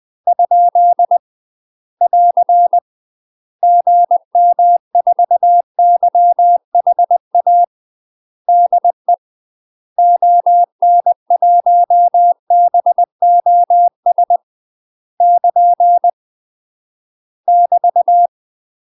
Morse code, heard as a keyed tone, about 20 wpm, 700 Hz. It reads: ? <AR> GM4YHA DE ON1BOS <KN> <BT>
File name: UK MASTER QSOs TRAINING / 20WPM / UK MASTER QSOs 0058